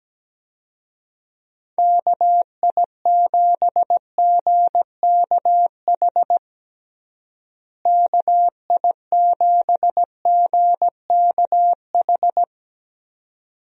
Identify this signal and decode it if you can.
Morse code, heard as a keyed tone, about 17 wpm, 700 Hz. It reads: KI7GKH KI7GKH
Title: USA Call Signs - 6 Characters - Repeated 2x - 17wpm